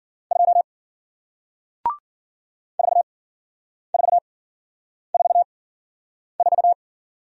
Morse code, heard as a keyed tone, about 55 wpm, spaced as for 9 wpm, 700 Hz. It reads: 2 4444